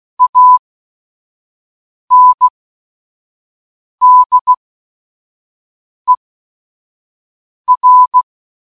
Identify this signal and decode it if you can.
Morse code, heard as a keyed tone, about 16 wpm, spaced as for 5 wpm, 1000 Hz. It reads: ANDER
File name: escaperoom-019-morsecode-charmander